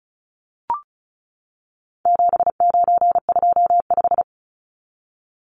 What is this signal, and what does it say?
Morse code, heard as a keyed tone, about 34 wpm, 700 Hz. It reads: 7925
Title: Numbers - 4-Digit 35wpm